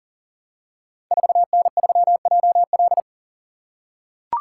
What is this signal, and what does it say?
Morse code, heard as a keyed tone, about 39 wpm, 700 Hz. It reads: 4N3JL
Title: Code Groups - 5 Letter:Numbers - 40wpm